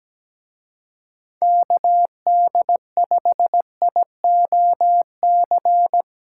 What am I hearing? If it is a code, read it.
Morse code, heard as a keyed tone, about 20 wpm, 700 Hz. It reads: KD5IOC
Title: Speed Racing - USA 2x3 Call Signs 17wpm